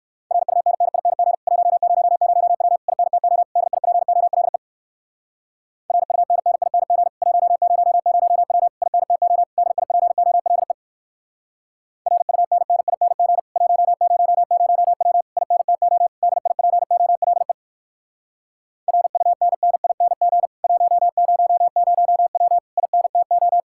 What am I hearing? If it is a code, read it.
Morse code, heard as a keyed tone, about 58 wpm, 700 Hz. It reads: RUNNING 100W INTO DIPOLE RUNNING 100W INTO DIPOLE RUNNING 100W INTO DIPOLE RUNNING 100W INTO